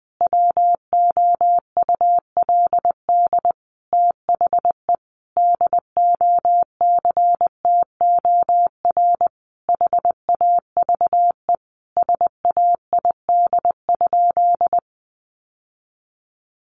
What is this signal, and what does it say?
Morse code, heard as a keyed tone, about 19 wpm, 700 Hz. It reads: WOULD THE DOCTOR HAVE SAID?
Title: Sentences from Top 1500 Words 20wpm